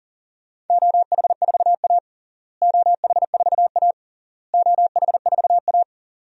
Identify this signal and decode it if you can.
Morse code, heard as a keyed tone, about 40 wpm, 700 Hz. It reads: OH4A OH4A OH4A
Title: DXCC-100 Call Signs - 4 Characters - Repeated 3x - 40wpm